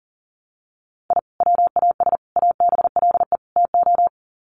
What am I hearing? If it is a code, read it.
Morse code, heard as a keyed tone, about 40 wpm, 700 Hz. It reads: I WAS ABLE TO
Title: Sets of 4 Words 40wpm